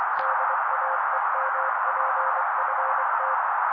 Morse code, heard as a keyed tone, about 24 wpm, 530 Hz. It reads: DAEMPFT
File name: daily / 23wpm noise